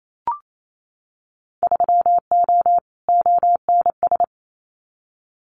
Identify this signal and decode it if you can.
Morse code, heard as a keyed tone, about 28 wpm, 700 Hz. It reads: 3O ONS
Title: Winter Field Day Exchanges Encore 28wpm